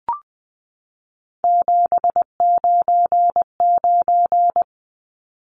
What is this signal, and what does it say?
Morse code, heard as a keyed tone, about 20 wpm, 700 Hz. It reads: 799
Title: Numbers - 3-Digit 20wpm